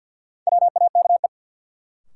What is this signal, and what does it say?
Morse code, heard as a keyed tone, about 52 wpm, 700 Hz. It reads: WAKE
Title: Sets of 4 Words - New Word Drill - 50wpm